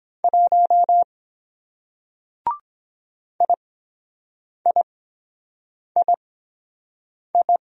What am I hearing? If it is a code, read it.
Morse code, heard as a keyed tone, about 22 wpm, spaced as for 7 wpm, 700 Hz. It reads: 1 IIII